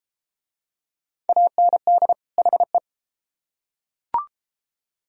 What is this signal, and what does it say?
Morse code, heard as a keyed tone, about 29 wpm, 700 Hz. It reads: AND HE